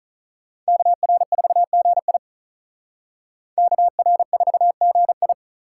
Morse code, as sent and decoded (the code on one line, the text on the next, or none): -.- .-. ....- --. .. / -.- .-. ....- --. ..
KR4GI KR4GI